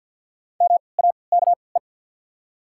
Morse code, heard as a keyed tone, about 54 wpm, spaced as for 29 wpm, 690 Hz. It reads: MAKE
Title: 3x Character Spacing - Top 100 Words - 50wpm